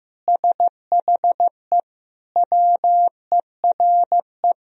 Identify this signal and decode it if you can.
Morse code, heard as a keyed tone, about 15 wpm, 700 Hz. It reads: SHE WERE
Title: Sentences from Top 1000 Words - Review Part 1 - 15wpm